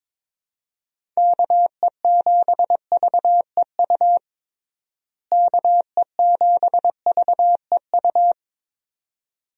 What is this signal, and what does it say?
Morse code, heard as a keyed tone, about 22 wpm, 700 Hz. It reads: KE7VEU KE7VEU
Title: USA Call Signs - 6 Characters - Repeated 2x - 22wpm